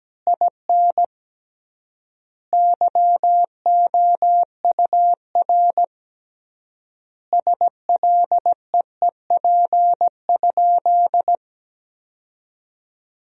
Morse code, heard as a keyed tone, about 17 wpm, 700 Hz. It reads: IN YOUR SLEEP?
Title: 3x Word-Spacing - Sentences from Top 500 Words - 17wpm